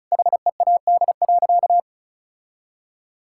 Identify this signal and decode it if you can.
Morse code, heard as a keyed tone, about 36 wpm, 700 Hz. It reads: HEAD.
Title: Sentences from Top 1600 Words 35wpm